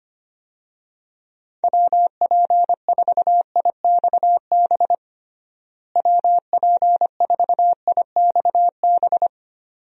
Morse code, heard as a keyed tone, about 25 wpm, 700 Hz. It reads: WP4IXB WP4IXB
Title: USA Call Signs - 6 Characters - Repeated 2x - 25wpm